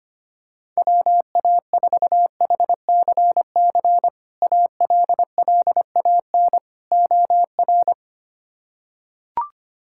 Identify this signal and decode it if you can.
Morse code, heard as a keyed tone, about 25 wpm, 700 Hz. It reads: WA4HCC ALLAN OR
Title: K1USN SST QSO Practice - 25wpm